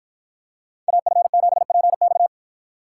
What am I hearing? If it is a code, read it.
Morse code, heard as a keyed tone, about 50 wpm, 700 Hz. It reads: AF7PX